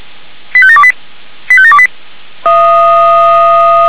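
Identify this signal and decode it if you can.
Five-tone selective call (ZVEI), heard as quick runs of five tones, tones of about 70 ms. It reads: 85728 85728
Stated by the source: ZVEI five-tone call 85724 received by SDR, logged on the audio server of feuerwehr-orlamuende.org.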